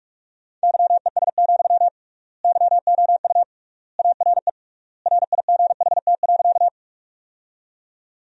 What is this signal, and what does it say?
Morse code, heard as a keyed tone, about 46 wpm, 700 Hz. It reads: YES, YOU ARE RIGHT.
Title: 3x Word-Spacing - Sentences from Top 500 Words - 45wpm